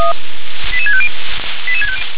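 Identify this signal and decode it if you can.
Five-tone selective call (ZVEI), heard as quick runs of five tones, tones of about 70 ms. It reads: 88650
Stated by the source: ZVEI five-tone selective call received on an SDR.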